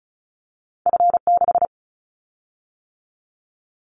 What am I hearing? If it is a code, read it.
Morse code, heard as a keyed tone, about 36 wpm, 700 Hz. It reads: F6
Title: Callsign Prefixes for Top 50 DXCC - 35wpm